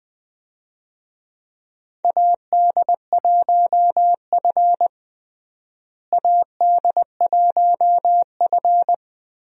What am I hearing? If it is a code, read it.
Morse code, heard as a keyed tone, about 21 wpm, 700 Hz. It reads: AD1F AD1F
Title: USA Call Signs - 4 Characters - Repeated 3x - 20wpm